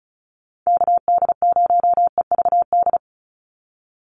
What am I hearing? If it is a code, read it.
Morse code, heard as a keyed tone, about 34 wpm, 700 Hz. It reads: KD0EVD
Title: USA Call Signs 35wpm